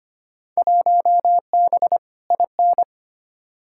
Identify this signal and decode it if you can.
Morse code, heard as a keyed tone, about 25 wpm, 700 Hz. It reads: 1B IN
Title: Field Day Exchanges - 25wpm